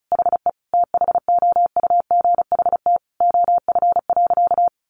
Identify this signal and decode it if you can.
Morse code, heard as a keyed tone, about 34 wpm, 700 Hz. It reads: HE THOUGHT OF.
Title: Sentences from Top 300 Words 35wpm